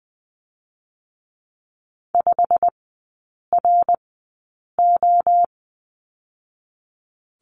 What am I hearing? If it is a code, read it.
Morse code, heard as a keyed tone, about 20 wpm, spaced as for 8 wpm, 690 Hz. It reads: 5RO
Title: 087 Lesson 15e - Number 5 - Three Characters 20wpm